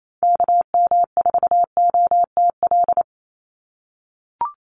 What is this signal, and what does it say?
Morse code, heard as a keyed tone, about 28 wpm, 700 Hz. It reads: KM4OTL
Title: USA Call Signs 28wpm